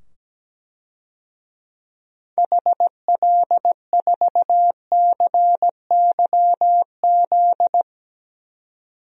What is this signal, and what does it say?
Morse code, heard as a keyed tone, about 17 wpm, 700 Hz. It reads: HL4CYZ
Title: DXCC-100 Call Signs - 6 Characters - 17wpm